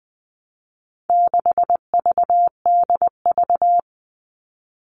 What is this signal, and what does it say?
Morse code, heard as a keyed tone, about 20 wpm, 700 Hz. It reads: 6VDV
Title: All Cut Numbers - 4-Digit 20wpm